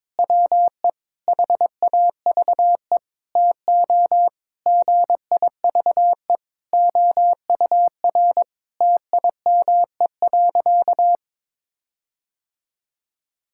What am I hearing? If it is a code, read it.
Morse code, heard as a keyed tone, about 22 wpm, 700 Hz. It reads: WE HAVE TO GIVE OUR TIME.
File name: practice / 22wpm / Sentences with 4-Word Phrases - 22wpm